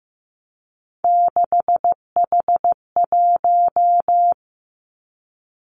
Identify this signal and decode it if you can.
Morse code, heard as a keyed tone, about 15 wpm, 700 Hz. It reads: 6H1
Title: Three Challenging Letter-Number Combinations 15wpm